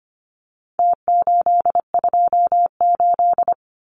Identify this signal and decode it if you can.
Morse code, heard as a keyed tone, about 25 wpm, 700 Hz. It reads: T828
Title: Numbers - Contest Style - 4-Digit 25wpm